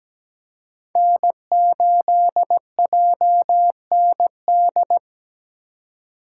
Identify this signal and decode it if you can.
Morse code, heard as a keyed tone, about 17 wpm, 700 Hz. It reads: N8JND
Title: USA Call Signs - 4x Character Spacing - 17wpm